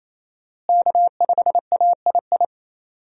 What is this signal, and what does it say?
Morse code, heard as a keyed tone, about 27 wpm, 700 Hz. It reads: K5AII
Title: USA Call Signs - 5 Characters - Repeated 2x - 28wpm